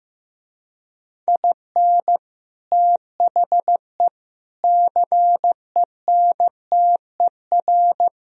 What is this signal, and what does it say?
Morse code, heard as a keyed tone, about 15 wpm, 700 Hz. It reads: IN THE CENTER
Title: Speed-Racing - Sets of 4 Words 15wpm